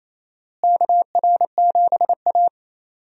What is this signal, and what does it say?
Morse code, heard as a keyed tone, about 28 wpm, 700 Hz. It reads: KR7A
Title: USA Call Signs - 4 Characters - Repeated 2x - 28wpm